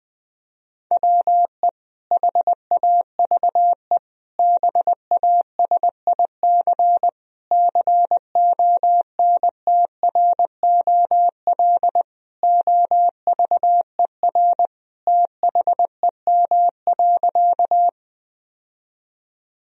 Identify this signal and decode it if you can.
Morse code, heard as a keyed tone, about 20 wpm, 700 Hz. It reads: WE HAVE BASIC CONTROL OVER THEM.
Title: Sentences from Top 1000 Words - Review Part 3 - 20wpm